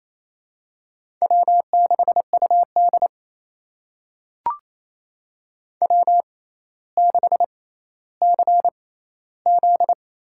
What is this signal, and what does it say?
Morse code, heard as a keyed tone, about 28 wpm, spaced as for 14 wpm, 700 Hz. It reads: W6UD W6CZ
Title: USA Call Signs - 6x Character Spacing - 28wpm